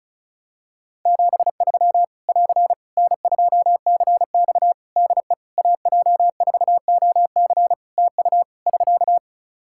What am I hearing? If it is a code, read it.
Morse code, heard as a keyed tone, about 34 wpm, 700 Hz. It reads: 73 <AR> N2CX DE AJ4OC TU <SK>
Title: Practice QSO - Exchange 4 - Ending - 35wpm